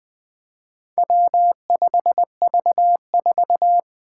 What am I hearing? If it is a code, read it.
Morse code, heard as a keyed tone, about 20 wpm, 700 Hz. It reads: W5V4
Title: Four Challenging Letter-Number Combinations 20wpm